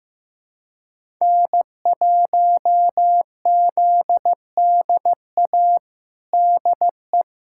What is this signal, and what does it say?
Morse code, heard as a keyed tone, about 15 wpm, 700 Hz. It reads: N1ZDA DE
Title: USA Callsign DE Callsign - 5 Characters - 15wpm